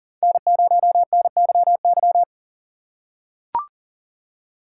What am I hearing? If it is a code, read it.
Morse code, heard as a keyed tone, about 40 wpm, 700 Hz. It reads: N0NYY
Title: USA Call Signs - 5 Characters - 6x Character Spacing - 40wpm